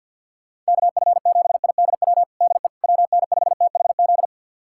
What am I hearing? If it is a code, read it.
Morse code, heard as a keyed tone, about 52 wpm, 700 Hz. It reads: KF7IDW DE WN5THZ
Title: USA Callsign DE Callsign - 6 Characters - 50wpm